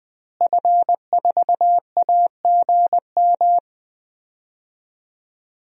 Morse code, heard as a keyed tone, about 20 wpm, 700 Hz. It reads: F4AGM
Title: DXCC-100 Call Signs - 5 Characters - Repeated 3x - 20wpm